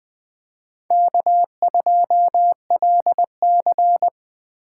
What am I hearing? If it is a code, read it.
Morse code, heard as a keyed tone, about 20 wpm, 700 Hz. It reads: K2LC
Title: USA Call Signs - 4 Characters - 2x Character Spacing - 20wpm